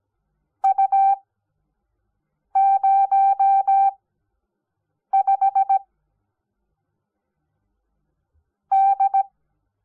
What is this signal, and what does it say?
Morse code, heard as a keyed tone, about 17 wpm, 780 Hz. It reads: U 0 5 D
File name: Onlinelehrgang / morsen / DJ4UF Morsen05